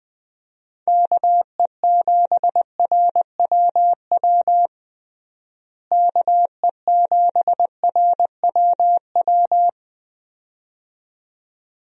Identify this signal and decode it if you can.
Morse code, heard as a keyed tone, about 20 wpm, 700 Hz. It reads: KE7RWW KE7RWW
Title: USA Call Signs - 6 Characters - Repeated 2x - 20wpm